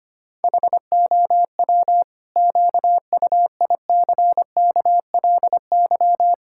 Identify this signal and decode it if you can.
Morse code, heard as a keyed tone, about 25 wpm, 700 Hz. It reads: HOW QUICKLY
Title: Pangram Challenge v2 - 25wpm